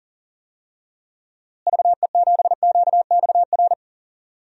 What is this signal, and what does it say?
Morse code, heard as a keyed tone, about 40 wpm, 700 Hz. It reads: VE7QXR
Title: DXCC-100 Call Signs - 6 Characters - Repeated 2x - 40wpm